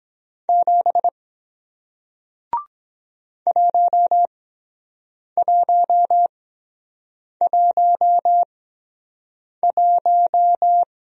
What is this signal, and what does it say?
Morse code, heard as a keyed tone, about 24 wpm, spaced as for 7 wpm, 700 Hz. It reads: 7 1111